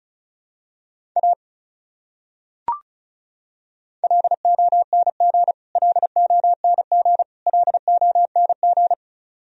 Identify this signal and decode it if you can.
Morse code, heard as a keyed tone, about 34 wpm, 700 Hz. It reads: A LONG LONG LONG
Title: Top 100 Words Repeated 35wpm